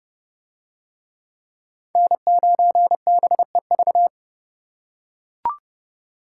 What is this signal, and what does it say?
Morse code, heard as a keyed tone, about 28 wpm, 700 Hz. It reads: N9BEV